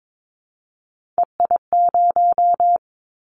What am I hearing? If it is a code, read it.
Morse code, heard as a keyed tone, about 22 wpm, 700 Hz. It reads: EI0
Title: Callsign Prefixes for Top 50 DXCC - 22wpm